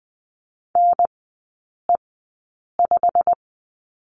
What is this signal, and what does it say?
Morse code, heard as a keyed tone, about 20 wpm, spaced as for 8 wpm, 700 Hz. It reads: NE5
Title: 107 Lesson 18e - Letter m - Three Characters 20wpm